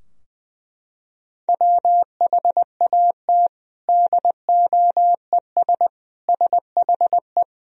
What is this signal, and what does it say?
Morse code, heard as a keyed tone, about 20 wpm, 700 Hz. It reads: WHAT DOES SHE
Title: Sentences from Top 1000 Words - Review Part 4 - 20wpm